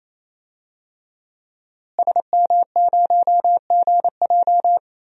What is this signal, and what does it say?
Morse code, heard as a keyed tone, about 28 wpm, 700 Hz. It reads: SM0GJ
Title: DXCC-100 Call Signs - 5 Characters - Repeated 3x - 28wpm